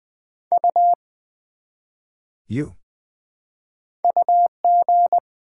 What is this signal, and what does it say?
Morse code, heard as a keyed tone, about 20 wpm, 700 Hz. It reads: U UG
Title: Build A Word - 1 Letter At A Time - 4-Letter Words - 20wpm